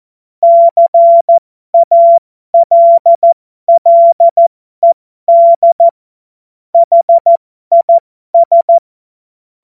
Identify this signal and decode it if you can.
Morse code, heard as a keyed tone, about 14 wpm, spaced as for 12 wpm, 680 Hz. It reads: CALLED HIS
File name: Morsekurs / Witze